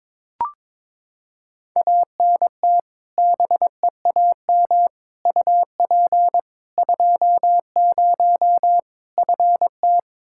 Morse code, heard as a keyed tone, about 22 wpm, 700 Hz. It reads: ANT BEAM UP 20 FT
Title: K1USN SST QSO Practice 22wpm